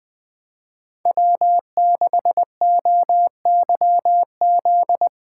Morse code, heard as a keyed tone, about 20 wpm, 700 Hz. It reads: W6OYZ
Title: Speed Racing - USA 1x3 Call Signs 17wpm